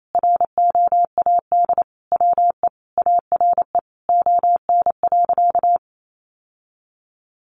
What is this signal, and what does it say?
Morse code, heard as a keyed tone, about 28 wpm, 700 Hz. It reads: ROAD WE ARE ON.
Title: Sentences from Top 1100 Words 28wpm